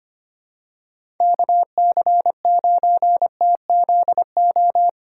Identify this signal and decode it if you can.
Morse code, heard as a keyed tone, about 25 wpm, 700 Hz. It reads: KC9TZO
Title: K1USN SST QSO Practice - 25wpm